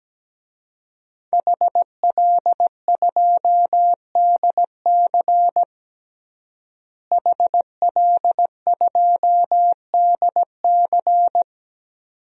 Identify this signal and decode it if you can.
Morse code, heard as a keyed tone, about 17 wpm, 700 Hz. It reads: HL2DC HL2DC
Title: DXCC-100 Call Signs - 5 Characters - Repeated 2x - 17wpm